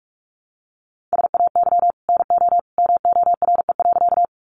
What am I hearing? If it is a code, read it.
Morse code, heard as a keyed tone, about 44 wpm, 700 Hz. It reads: SAY NO MORE.
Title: Sentences from Top 200 Words 45wpm